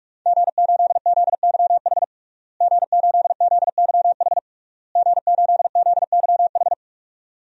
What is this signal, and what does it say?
Morse code, heard as a keyed tone, about 46 wpm, 700 Hz. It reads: G8ZYH G8ZYH G8ZYH
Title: DXCC-100 Call Signs - 5 Characters - Repeated 3x - 45wpm